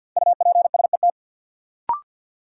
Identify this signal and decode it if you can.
Morse code, heard as a keyed tone, about 52 wpm, 700 Hz. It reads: UPSET
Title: Sentences from Top 800 Words - New Word Drill - 50wpm